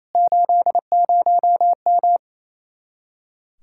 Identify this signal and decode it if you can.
Morse code, heard as a keyed tone, about 31 wpm, 700 Hz. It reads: 80M
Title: Speed-Racing Common QSO Phrases - 28wpm